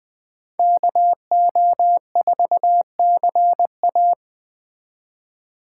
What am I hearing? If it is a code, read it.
Morse code, heard as a keyed tone, about 20 wpm, 700 Hz. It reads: KO4CA
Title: Speed Racing - USA Call Signs 17wpm